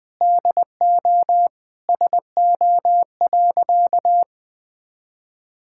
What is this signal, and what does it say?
Morse code, heard as a keyed tone, about 20 wpm, 700 Hz. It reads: DO SO.
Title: Sentences from Top 1000 Words - Review Part 6 - 20wpm